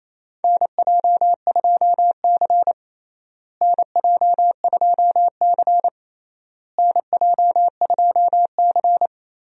Morse code, heard as a keyed tone, about 28 wpm, 700 Hz. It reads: NJ2C NJ2C NJ2C
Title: USA Call Signs - 4 Characters - Repeated 3x - 28wpm